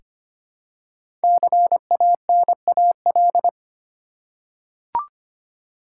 Morse code, heard as a keyed tone, about 25 wpm, 700 Hz. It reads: CANAL
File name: 5 Letter Words - First 3 Letters Given - 25wpm